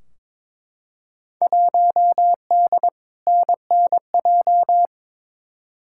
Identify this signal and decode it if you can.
Morse code, heard as a keyed tone, about 22 wpm, 700 Hz. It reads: 1D NNJ
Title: Field Day Exchanges Encore - 22wpm